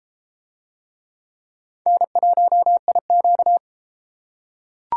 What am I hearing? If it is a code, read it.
Morse code, heard as a keyed tone, about 31 wpm, 700 Hz. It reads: N1IQ